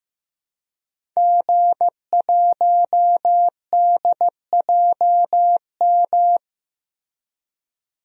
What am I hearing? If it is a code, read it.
Morse code, heard as a keyed tone, about 15 wpm, 700 Hz. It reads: G1DJM
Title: DXCC-100 Call Signs - 5 Characters - Repeated 2x - 15wpm